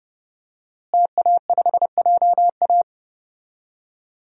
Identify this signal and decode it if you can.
Morse code, heard as a keyed tone, about 30 wpm, 700 Hz. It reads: TA5JA
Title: DXCC-100 Call Signs - 5 Characters - Repeated 2x - 30wpm